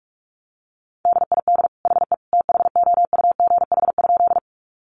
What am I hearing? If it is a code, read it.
Morse code, heard as a keyed tone, about 46 wpm, 700 Hz. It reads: DID HE THOUGH?
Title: Sentences from Top 400 Words 45wpm